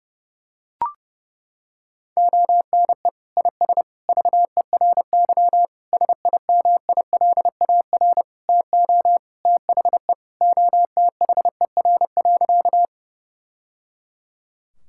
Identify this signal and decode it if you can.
Morse code, heard as a keyed tone, about 29 wpm, 700 Hz. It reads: ONE IS VERY SIMILAR TO THE OTHER.
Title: Sentences from Top 1000 Words - Review Part 5 - 30wpm